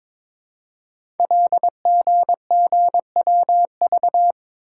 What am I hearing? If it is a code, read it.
Morse code, heard as a keyed tone, about 22 wpm, 690 Hz. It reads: LGGWV
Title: Code Groups - 5 Letters - 22wpm